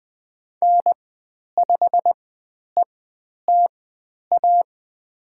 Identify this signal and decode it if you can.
Morse code, heard as a keyed tone, about 20 wpm, spaced as for 10 wpm, 700 Hz. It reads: N5ETA
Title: 102 Lesson 17g - Letter u - Call Signs 20wpm